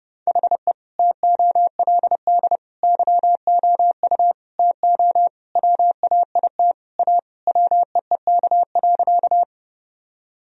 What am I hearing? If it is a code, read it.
Morse code, heard as a keyed tone, about 32 wpm, 700 Hz. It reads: HE TOLD YOU TO WAIT A WEEK.
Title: Sentences from Top 1000 Words - Review Part 2 - 30wpm